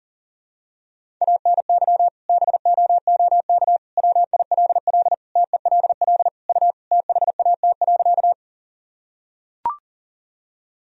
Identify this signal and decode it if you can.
Morse code, heard as a keyed tone, about 37 wpm, 700 Hz. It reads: ANY BOOK WILL TELL U THAT.
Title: Sentences from Top 300 Words and Abbreviations 40wpm